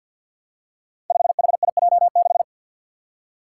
Morse code, heard as a keyed tone, about 50 wpm, 700 Hz. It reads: 5HIJ6
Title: Five Challenging Letter-Number Combinations 50wpm